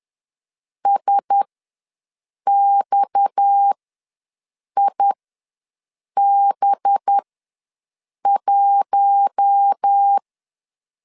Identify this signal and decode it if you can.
Morse code, heard as a keyed tone, about 10 wpm, spaced as for 6 wpm, 800 Hz. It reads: SXIB1
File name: LEZIONI / 8 YJ45 / 8 YJ4 ws 10wpm 4020